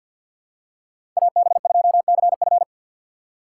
Morse code, heard as a keyed tone, about 46 wpm, 700 Hz. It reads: AB2CF
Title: USA Call Signs - 5 Characters - 6x Character Spacing - 50wpm